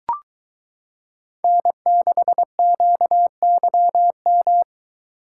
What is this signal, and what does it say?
Morse code, heard as a keyed tone, about 22 wpm, 700 Hz. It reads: N6QYM